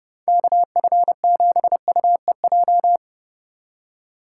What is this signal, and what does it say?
Morse code, heard as a keyed tone, about 30 wpm, 700 Hz. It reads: KF7UEJ
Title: USA Call Signs - 6 Characters - 8x Character Spacing - 30wpm